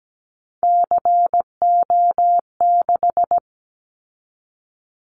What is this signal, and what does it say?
Morse code, heard as a keyed tone, about 17 wpm, 700 Hz. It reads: CO6
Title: Callsign Prefixes for Top 100 DXCC - 17wpm